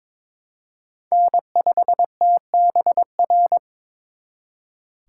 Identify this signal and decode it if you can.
Morse code, heard as a keyed tone, about 26 wpm, 700 Hz. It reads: N5TBR
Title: Speed Racing - USA 1x3 Call Signs 22wpm